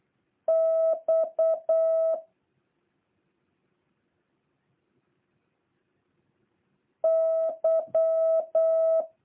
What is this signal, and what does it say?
Morse code, heard as a keyed tone, about 8 wpm, 650 Hz. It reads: X Y